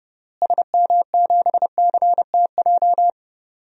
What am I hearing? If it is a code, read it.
Morse code, heard as a keyed tone, about 30 wpm, 700 Hz. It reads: SM7CTJ
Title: DXCC-100 Call Signs - 6 Characters - Repeated 2x - 30wpm